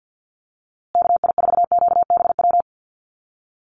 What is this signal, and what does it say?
Morse code, heard as a keyed tone, about 50 wpm, 700 Hz. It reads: KI4QBW
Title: USA Call Signs Repeated 2x 50wpm